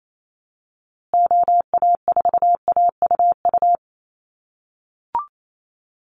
Morse code, heard as a keyed tone, about 28 wpm, 700 Hz. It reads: OA4AUU
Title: USA Call Signs Repeated 2x 28wpm